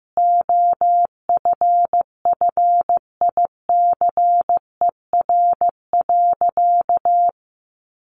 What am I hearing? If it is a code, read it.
Morse code, heard as a keyed tone, about 15 wpm, 700 Hz. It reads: OFFICER.
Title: Sentences from Top 1200 Words 15wpm